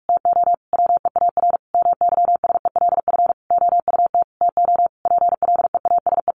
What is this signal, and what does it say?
Morse code, heard as a keyed tone, about 44 wpm, 700 Hz. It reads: TO WEAR MYSELF OUT TO PLEASE
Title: Sentences from Top 1500 Words 45wpm